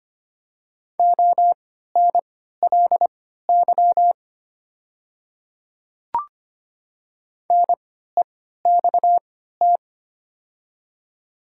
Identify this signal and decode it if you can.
Morse code, heard as a keyed tone, about 25 wpm, spaced as for 14 wpm, 700 Hz. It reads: ONLY NEXT
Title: 3x Character Spacing - Top 300 Words - 25wpm